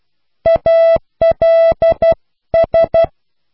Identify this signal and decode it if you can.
Morse code, heard as a keyed tone, about 12 wpm, 660 Hz. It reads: ALS